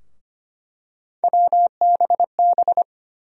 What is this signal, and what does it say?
Morse code, heard as a keyed tone, about 25 wpm, 700 Hz. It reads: WBB
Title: USA Call Sign Postfixes - 3 Letters 25wpm